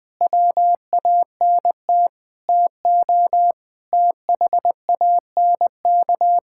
Morse code, heard as a keyed tone, about 20 wpm, 700 Hz. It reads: WANT TO THANK
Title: Sentences with 8-Word Phrases - 20wpm